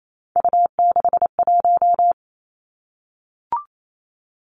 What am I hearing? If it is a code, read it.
Morse code, heard as a keyed tone, about 28 wpm, 700 Hz. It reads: U61
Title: Three Challenging Letter-Number Combinations 28wpm